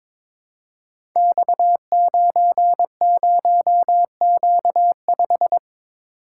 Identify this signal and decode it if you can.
Morse code, heard as a keyed tone, about 23 wpm, 700 Hz. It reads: X90Q5
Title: Code Groups - 5 Letter:Numbers - 22wpm